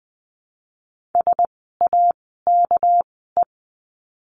Morse code, heard as a keyed tone, about 20 wpm, spaced as for 14 wpm, 700 Hz. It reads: SAKE
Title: 199 Lesson 32f - Letter k - Words 20wpm